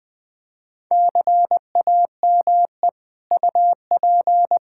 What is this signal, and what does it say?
Morse code, heard as a keyed tone, about 20 wpm, 700 Hz. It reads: CAME UP
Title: Sets of 6 Words - 20wpm